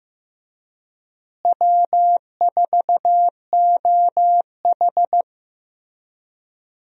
Morse code, heard as a keyed tone, about 16 wpm, 700 Hz. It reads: W4OH
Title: USA Call Signs - 4 Characters - Repeated 3x - 15wpm